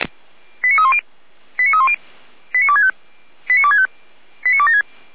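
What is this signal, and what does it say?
Five-tone selective call (ZVEI), heard as quick runs of five tones, tones of about 70 ms. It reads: 89310 89310 89265 89265 89276
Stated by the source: ZVEI five-tone selective call 89160 received by SDR at feuerwehr-orlamuende (fire brigade Orlamünde).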